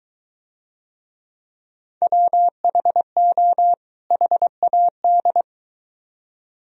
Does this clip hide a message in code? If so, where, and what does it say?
Yes: Morse code from 2.0 to 5.4 s, heard as a keyed tone, about 22 wpm, 700 Hz. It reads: WHO HAD